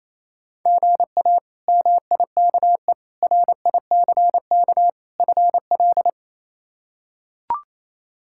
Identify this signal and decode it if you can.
Morse code, heard as a keyed tone, about 28 wpm, 700 Hz. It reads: GA MIKE RICK FL
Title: K1USN SST QSO Practice - 28wpm